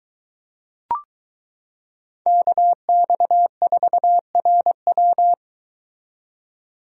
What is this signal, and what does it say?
Morse code, heard as a keyed tone, about 22 wpm, 700 Hz. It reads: KX4RW